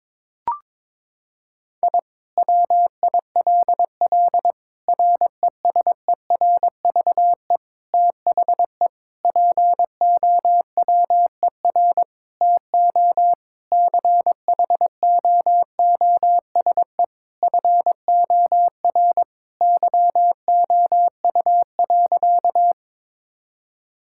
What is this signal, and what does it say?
Morse code, heard as a keyed tone, about 22 wpm, 700 Hz. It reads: I WILL RESERVE THE POWER TO CHOOSE FOR YOU.
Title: Sentences from Top 1000 Words - Review Part 6 - 22wpm